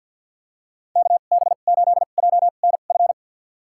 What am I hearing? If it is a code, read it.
Morse code, heard as a keyed tone, about 50 wpm, spaced as for 36 wpm, 690 Hz. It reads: KB8JNF
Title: USA Call Signs - 2x Character Spacing - 50wpm